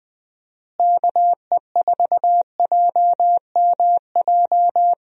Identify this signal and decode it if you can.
Morse code, heard as a keyed tone, about 20 wpm, 700 Hz. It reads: KE4JMJ
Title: USA Call Signs - 6 Characters - Repeated 2x - 20wpm